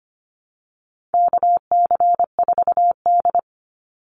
Morse code, heard as a keyed tone, about 25 wpm, 700 Hz. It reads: KC4D
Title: Call Signs Repeated 3x - 25wpm